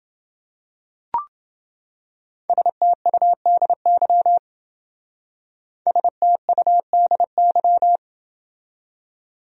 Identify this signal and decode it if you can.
Morse code, heard as a keyed tone, about 29 wpm, 700 Hz. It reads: STUDY STUDY